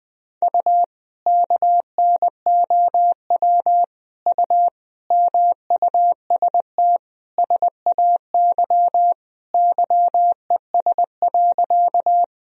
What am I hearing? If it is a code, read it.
Morse code, heard as a keyed tone, about 20 wpm, 700 Hz. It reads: U KNOW U MUST SAY YES.
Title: Sentences from Top 500 Words and Abbreviations 20wpm